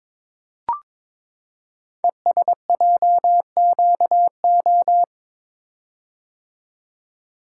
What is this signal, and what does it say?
Morse code, heard as a keyed tone, about 22 wpm, 700 Hz. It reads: ESJQO
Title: Code Groups - 5 Letters - 22wpm